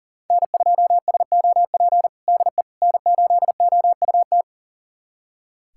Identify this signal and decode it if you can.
Morse code, heard as a keyed tone, about 40 wpm, 700 Hz. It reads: N2SOP DE N8OUT
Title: USA Callsign DE Callsign - 5 Characters - 40wpm